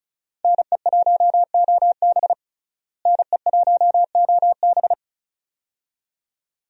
Morse code, heard as a keyed tone, about 34 wpm, 700 Hz. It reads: NE1OB NE1OB
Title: USA Call Signs - 5 Characters - Repeated 2x - 35wpm